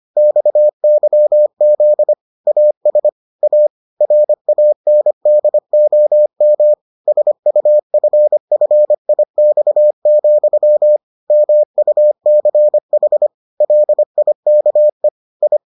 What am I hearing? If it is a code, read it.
Morse code, heard as a keyed tone, about 25 wpm, 600 Hz. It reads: XYZ AS A RANDOM SUFFIX, MUCH LIKE I